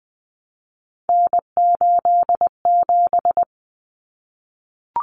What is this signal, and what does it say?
Morse code, heard as a keyed tone, about 20 wpm, 700 Hz. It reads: N87
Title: Common Cut Numbers - 3-Digit 20wpm